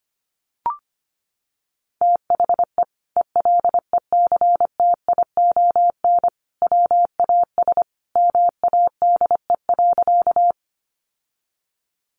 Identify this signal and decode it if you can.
Morse code, heard as a keyed tone, about 25 wpm, 700 Hz. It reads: THE ELECTION WAS MADE.
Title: Sentences from Top 1500 Words 25wpm